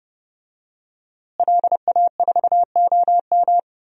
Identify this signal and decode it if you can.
Morse code, heard as a keyed tone, about 30 wpm, 700 Hz. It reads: LA4OM
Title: DXCC-100 Call Signs - 5 Characters - Repeated 2x - 30wpm